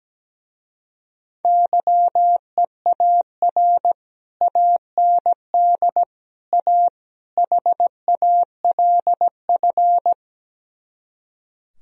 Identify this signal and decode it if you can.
Morse code, heard as a keyed tone, about 17 wpm, 700 Hz. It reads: YEAR AND A HALF
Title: Speed-Racing - Sets of 4 Words 17wpm